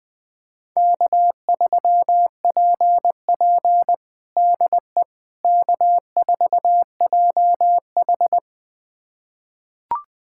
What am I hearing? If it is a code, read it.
Morse code, heard as a keyed tone, about 20 wpm, 700 Hz. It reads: K3PP DE K4JH
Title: USA Callsign DE Callsign - 4 Characters - 20wpm